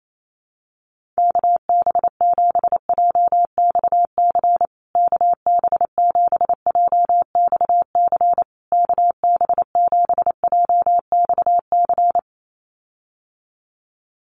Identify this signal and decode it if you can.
Morse code, heard as a keyed tone, about 28 wpm, 700 Hz. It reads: KB7JXC KB7JXC KB7JXC
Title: USA Call Signs Repeated 3x 28wpm